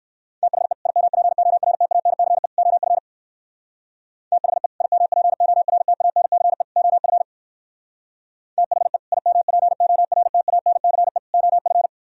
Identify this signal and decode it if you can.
Morse code, heard as a keyed tone, about 62 wpm, 700 Hz. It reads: THE IMPORTANCE OF THE IMPORTANCE OF THE IMPORTANCE OF